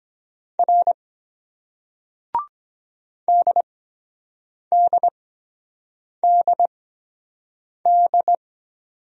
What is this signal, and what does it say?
Morse code, heard as a keyed tone, about 24 wpm, spaced as for 7 wpm, 700 Hz. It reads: R DDDD